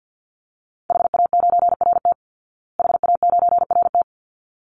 Morse code, heard as a keyed tone, about 49 wpm, 700 Hz. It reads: HA9RT HA9RT
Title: Call Signs Repeated 3x - 50wpm